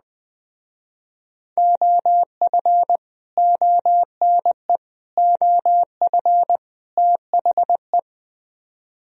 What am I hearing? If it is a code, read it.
Morse code, heard as a keyed tone, about 21 wpm, 700 Hz. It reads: OF ONE OF THE